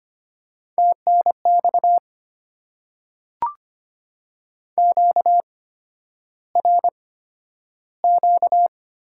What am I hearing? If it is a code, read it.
Morse code, heard as a keyed tone, about 25 wpm, spaced as for 11 wpm, 700 Hz. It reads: TNX QRQ
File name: Top 90 QSO Elements - 8x Character Spacing - 25wpm